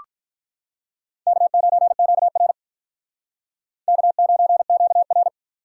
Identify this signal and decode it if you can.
Morse code, heard as a keyed tone, about 44 wpm, 700 Hz. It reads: K9QR K9QR